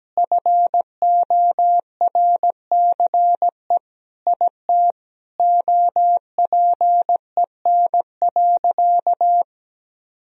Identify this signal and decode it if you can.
Morse code, heard as a keyed tone, about 17 wpm, 700 Hz. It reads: FORCE IT OPEN.
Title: Sentences from Top 1000 Words - Review Part 3 - 17wpm